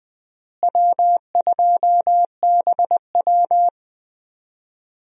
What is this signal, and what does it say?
Morse code, heard as a keyed tone, about 21 wpm, 700 Hz. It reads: W2BW
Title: USA Call Signs - 4 Characters - Repeated 3x - 20wpm